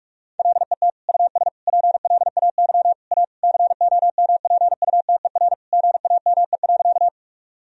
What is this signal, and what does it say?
Morse code, heard as a keyed tone, about 41 wpm, 700 Hz. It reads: LET US PLAY A COMPUTER GAME.
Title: Sentences from Top 1000 Words - Review Part 3 - 45wpm